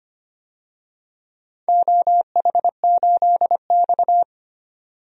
Morse code, heard as a keyed tone, about 25 wpm, 700 Hz. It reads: OH8X
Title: DXCC-100 Call Signs - 4 Characters - 25wpm